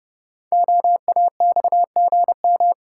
Morse code, heard as a keyed tone, about 29 wpm, 690 Hz. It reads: OAXGM
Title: Code Groups - 5 Letters - 30wpm